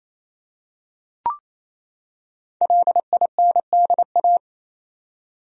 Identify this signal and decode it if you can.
Morse code, heard as a keyed tone, about 28 wpm, 700 Hz. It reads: LINDA
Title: USA Common Names - Max 5 Letters - 28wpm